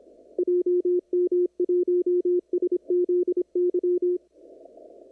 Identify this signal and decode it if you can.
Morse code, heard as a keyed tone, about 26 wpm, 360 Hz. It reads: JM1SZY
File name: CW / kt5j wpx cw-op-k5pi